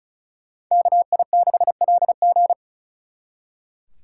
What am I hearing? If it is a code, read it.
Morse code, heard as a keyed tone, about 44 wpm, 700 Hz. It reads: KI6LG